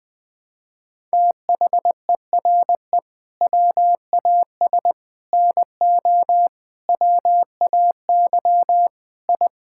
Morse code, heard as a keyed tone, about 21 wpm, 700 Hz. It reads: THERE WAS NO WAY I
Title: Sentences with 7-Word Phrases - 20wpm